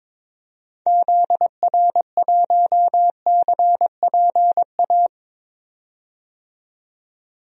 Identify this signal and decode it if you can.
Morse code, heard as a keyed tone, about 22 wpm, 700 Hz. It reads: ZR1CPA
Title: DXCC-100 Call Signs - 6 Characters - 22wpm